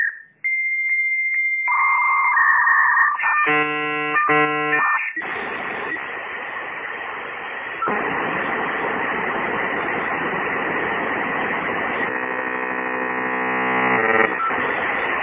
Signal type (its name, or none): dial-up modem handshake